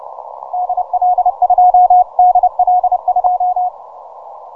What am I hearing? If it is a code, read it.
Morse code, heard as a keyed tone, about 30 wpm, 720 Hz. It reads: DL2 DL2
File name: simple-file-list / VP8SGI 12MCW